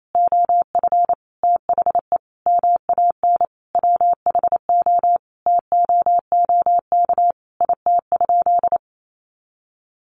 Morse code, heard as a keyed tone, about 28 wpm, 700 Hz. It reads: OF THE MAN WHO TOOK IT?
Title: Sentences from Top 300 Words 28wpm